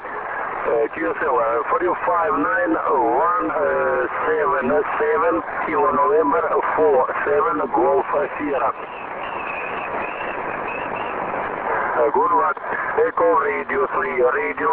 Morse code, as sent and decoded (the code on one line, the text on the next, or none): none